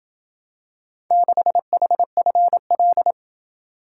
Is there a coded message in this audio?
Yes: Morse code from 1.1 to 3.1 s, heard as a keyed tone, about 30 wpm, 700 Hz. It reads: 6HFL